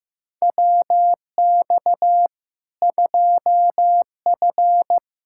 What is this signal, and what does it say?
Morse code, heard as a keyed tone, about 16 wpm, 700 Hz. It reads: WX 2F
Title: K1USN SST QSO Practice 15wpm